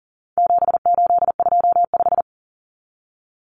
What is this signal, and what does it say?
Morse code, heard as a keyed tone, about 40 wpm, 700 Hz. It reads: 7825
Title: Numbers - 4-Digit 40wpm